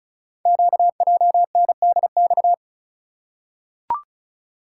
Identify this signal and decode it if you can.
Morse code, heard as a keyed tone, about 36 wpm, 700 Hz. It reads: QJNDX
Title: Code Groups - 5 Letters - 35wpm